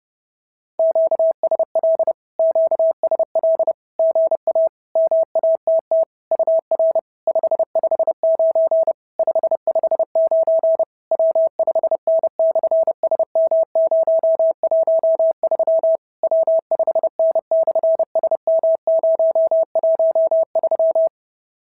Morse code, heard as a keyed tone, about 30 wpm, 690 Hz. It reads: QSL QSL GA MATT UR 559 559 W5N/SM013 W5N/SM013
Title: SOTA QSO Practice 30wpm